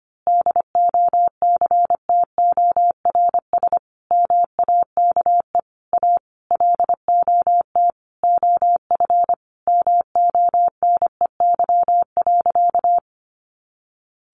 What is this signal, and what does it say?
Morse code, heard as a keyed tone, about 25 wpm, 700 Hz. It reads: DOCTORS MAKE A LOT OF MONEY.
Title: Easy Phrases 25wpm